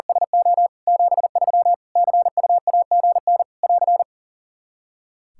Morse code, heard as a keyed tone, about 40 wpm, 700 Hz. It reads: SO 73 CUAGN <AR>
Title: Practice QSO - Exchange 4 - Ending - 40wpm